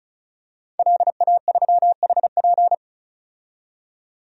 Morse code, heard as a keyed tone, about 34 wpm, 700 Hz. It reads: LA3HP
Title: DXCC-100 Call Signs - 5 Characters - 35wpm